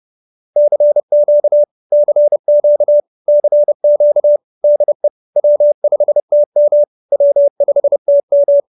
Morse code, heard as a keyed tone, about 30 wpm, 590 Hz. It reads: CQ CQ CQ DE W5TM W5TM